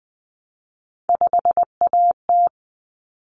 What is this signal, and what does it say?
Morse code, heard as a keyed tone, about 20 wpm, 700 Hz. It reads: 5AT
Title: Common Cut Numbers - 3-Digit 20wpm